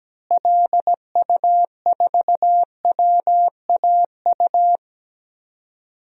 Morse code, heard as a keyed tone, about 17 wpm, 700 Hz. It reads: LU4WAU
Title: DXCC-100 Call Signs - 6 Characters - Repeated 3x - 17wpm